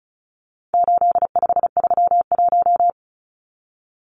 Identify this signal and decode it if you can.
Morse code, heard as a keyed tone, about 34 wpm, 700 Hz. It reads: 8531
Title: Numbers - 4-Digit 35wpm